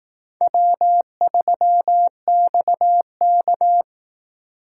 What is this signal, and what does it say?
Morse code, heard as a keyed tone, about 18 wpm, 700 Hz. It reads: W3XK
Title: Speed Racing - USA 1x2 Call Signs 15wpm